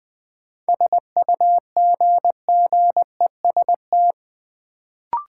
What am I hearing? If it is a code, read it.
Morse code, heard as a keyed tone, about 21 wpm, 700 Hz. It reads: SUGGEST
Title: Sentences from Top 1300 Words - New Word Drill Alt - 20wpm